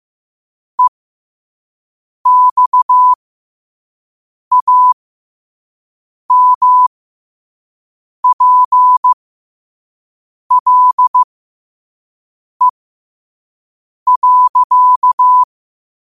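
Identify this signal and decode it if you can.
Morse code, heard as a keyed tone, about 15 wpm, spaced as for 5 wpm, 1000 Hz. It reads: EXAMPLE.
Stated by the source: Morse code practice at 5 wpm.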